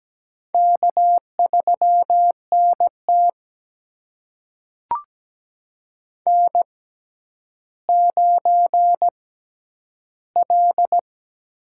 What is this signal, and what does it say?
Morse code, heard as a keyed tone, about 17 wpm, spaced as for 10 wpm, 700 Hz. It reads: K3NT N9L
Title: USA Call Signs - 6x Character Spacing - 17wpm